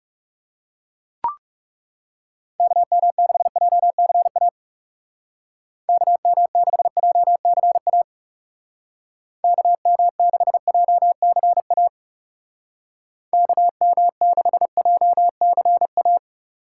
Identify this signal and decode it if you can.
Morse code, heard as a keyed tone, about 37 wpm, 700 Hz. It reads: KM6JCA KM6JCA KM6JCA KM6JCA
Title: Speed Racing - USA Call Signs 30wpm